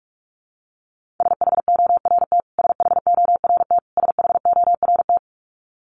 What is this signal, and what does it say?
Morse code, heard as a keyed tone, about 44 wpm, 700 Hz. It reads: SHORT SHORT SHORT
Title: Top 500 Words Repeated 45wpm